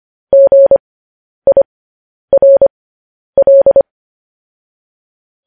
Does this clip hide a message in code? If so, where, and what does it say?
Yes: Morse code from 0.3 to 3.8 s, heard as a keyed tone, about 25 wpm, spaced as for 10 wpm, 550 Hz. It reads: GIRL